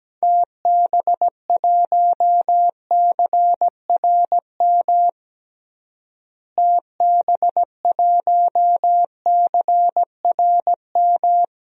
Morse code, heard as a keyed tone, about 17 wpm, 700 Hz. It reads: TB1CRM TB1CRM
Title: DXCC-100 Call Signs - 6 Characters - Repeated 3x - 17wpm